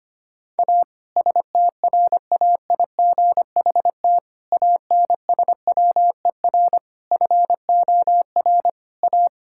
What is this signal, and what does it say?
Morse code, heard as a keyed tone, about 25 wpm, 700 Hz. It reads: A STRAIGHT ANSWER FOR A
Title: Sentences from Top 1000 Words - Review Part 6 - 25wpm